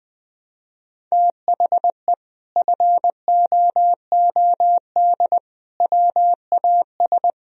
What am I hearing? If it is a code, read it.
Morse code, heard as a keyed tone, about 21 wpm, 700 Hz. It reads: THE FOOD WAS
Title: Sentences from Top 1000 Words - Review Part 1 - 20wpm